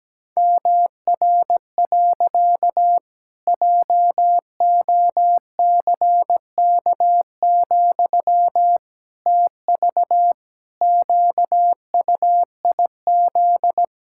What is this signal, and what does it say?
Morse code, heard as a keyed tone, about 17 wpm, 700 Hz. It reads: MR. JOCK, TV QUIZ
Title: Pangram Challenge v2 - 17wpm